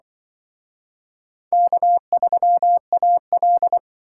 Morse code, heard as a keyed tone, about 25 wpm, 700 Hz. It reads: K3AL